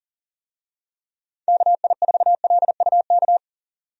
Morse code, heard as a keyed tone, about 40 wpm, 700 Hz. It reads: KI4LUK
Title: USA Call Signs - 6 Characters - 40wpm